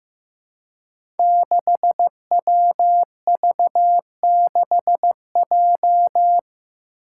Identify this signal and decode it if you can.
Morse code, heard as a keyed tone, about 15 wpm, 700 Hz. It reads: 6WV6J
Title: Five Challenging Letter-Number Combinations 15wpm